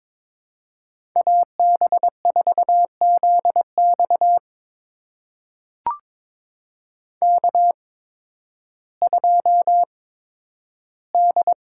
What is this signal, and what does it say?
Morse code, heard as a keyed tone, about 22 wpm, spaced as for 12 wpm, 700 Hz. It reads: AB4ZX K2D
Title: USA Call Signs - 5 Characters - 8x Character Spacing - 22wpm